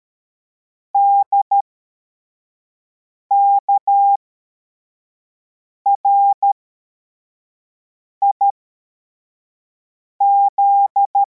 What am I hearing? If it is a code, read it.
Morse code, heard as a keyed tone, about 13 wpm, spaced as for 4 wpm, 800 Hz. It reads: DKRIZ
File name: formation / cw / Text64